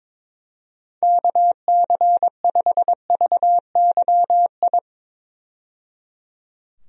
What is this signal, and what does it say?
Morse code, heard as a keyed tone, about 22 wpm, 700 Hz. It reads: KC5VYI
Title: USA Call Signs - 6 Characters - 22wpm